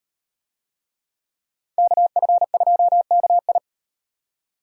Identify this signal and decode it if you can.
Morse code, heard as a keyed tone, about 35 wpm, 700 Hz. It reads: KF2KI